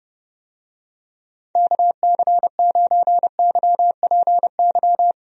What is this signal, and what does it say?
Morse code, heard as a keyed tone, about 30 wpm, 700 Hz. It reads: KC9YPY
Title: USA Call Signs - 6 Characters - Repeated 2x - 30wpm